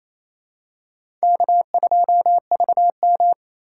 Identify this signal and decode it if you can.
Morse code, heard as a keyed tone, about 28 wpm, 700 Hz. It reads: K2VM
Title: USA Call Signs - 2x Character Spacing - 28wpm